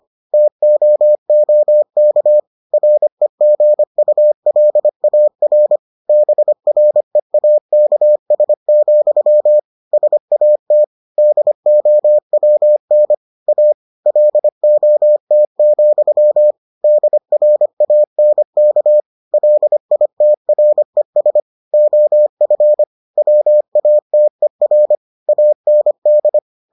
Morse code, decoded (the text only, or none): TOOK REGULAR BREAKS, SAT DOWN A LOT, DRANK LITRES OF WATER AND